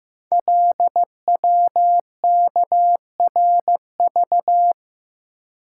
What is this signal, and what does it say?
Morse code, heard as a keyed tone, about 15 wpm, 700 Hz. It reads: LWKRV
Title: Code Groups - 5 Letters - 15wpm